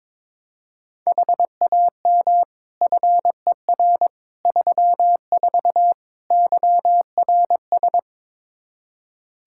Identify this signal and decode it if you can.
Morse code, heard as a keyed tone, about 22 wpm, 700 Hz. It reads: HAM FER 34 YRS
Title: Practice QSO - Exchange 3 - More Info - 22wpm